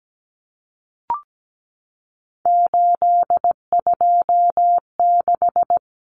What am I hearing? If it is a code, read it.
Morse code, heard as a keyed tone, about 17 wpm, 700 Hz. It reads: 826
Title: Common Cut Numbers - 3-Digit 17wpm